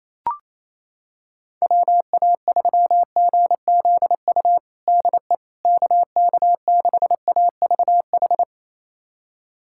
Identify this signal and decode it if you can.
Morse code, heard as a keyed tone, about 28 wpm, 700 Hz. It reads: WA3GZU DE KK6AVH
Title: USA Callsign DE Callsign - 6 Characters - 28wpm